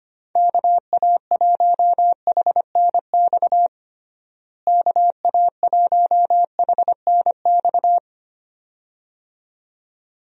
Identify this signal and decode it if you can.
Morse code, heard as a keyed tone, about 25 wpm, 700 Hz. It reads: KA1HNX KA1HNX
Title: USA Call Signs - 6 Characters - Repeated 2x - 25wpm